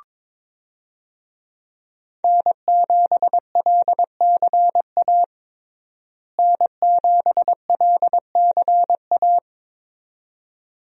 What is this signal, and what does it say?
Morse code, heard as a keyed tone, about 21 wpm, 700 Hz. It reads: N7LCA N7LCA
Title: USA Call Signs - 5 Characters - Repeated 2x - 22wpm